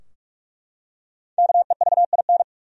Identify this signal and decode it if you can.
Morse code, heard as a keyed tone, about 44 wpm, 700 Hz. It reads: KEVIN
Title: USA Common Names - 45wpm